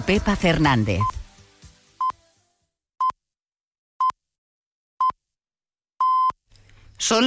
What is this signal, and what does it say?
Radio time signal (time pips), with speech. It is heard from 1.0 to 6.3 s.